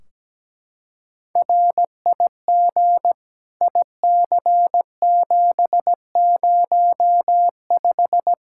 Morse code, heard as a keyed tone, about 17 wpm, 700 Hz. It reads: RIG IC705
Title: Practice QSO - Exchange 2 - Info - 17wpm